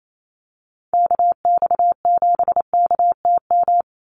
Morse code, heard as a keyed tone, about 28 wpm, 700 Hz. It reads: KX7KTM
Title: USA Call Signs 28wpm